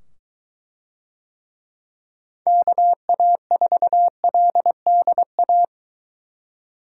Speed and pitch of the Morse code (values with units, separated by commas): 23 wpm, 700 Hz